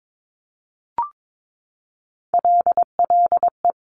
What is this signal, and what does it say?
Morse code, heard as a keyed tone, about 22 wpm, 700 Hz. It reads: LLE
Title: Top Three Letter Combinations in Words 22wpm